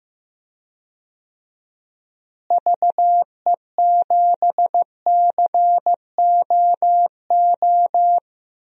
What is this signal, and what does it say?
Morse code, heard as a keyed tone, about 15 wpm, 700 Hz. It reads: VE7COO
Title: DXCC-100 Call Signs - 6 Characters - Repeated 3x - 15wpm